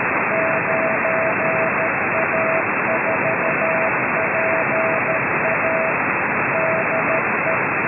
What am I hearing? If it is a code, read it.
Morse code, heard as a keyed tone, about 13 wpm, 650 Hz. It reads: 9A4PA DE